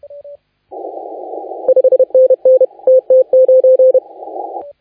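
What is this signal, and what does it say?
Morse code, heard as a keyed tone, about 30 wpm, 520 Hz. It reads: 5NN TT9
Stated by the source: Morse contest audio recorded by LA4C during the SAC CW 2018 contest.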